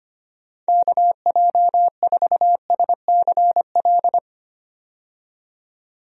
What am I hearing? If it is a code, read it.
Morse code, heard as a keyed tone, about 25 wpm, 700 Hz. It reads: KJ4SCL
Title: USA Call Signs - 6 Characters - 25wpm